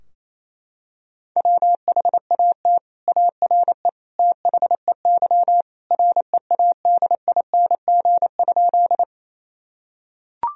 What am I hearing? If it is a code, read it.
Morse code, heard as a keyed tone, about 27 wpm, 700 Hz. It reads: WHAT ARE THEY READING?
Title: Sentences from Top 1000 Words - Review Part 3 - 28wpm